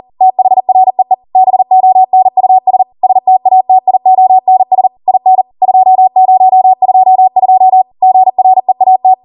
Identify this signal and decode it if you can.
Morse code, heard as a keyed tone, about 40 wpm, 750 Hz. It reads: THREE BONUS STATIONS IN 2022 GREAT